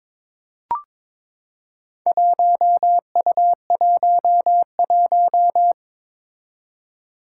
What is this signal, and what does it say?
Morse code, heard as a keyed tone, about 22 wpm, 700 Hz. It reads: 1U11
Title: Four Challenging Letter-Number Combinations 22wpm